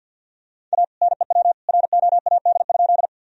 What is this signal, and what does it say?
Morse code, heard as a keyed tone, about 50 wpm, 700 Hz. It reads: A NEW ROAD?
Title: Sentences from Top 1000 Words - Review Part 2 - 50wpm